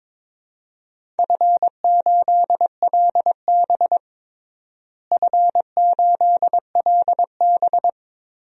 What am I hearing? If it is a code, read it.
Morse code, heard as a keyed tone, about 22 wpm, 700 Hz. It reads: F8LB F8LB
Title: DXCC-100 Call Signs - 4 Characters - Repeated 3x - 22wpm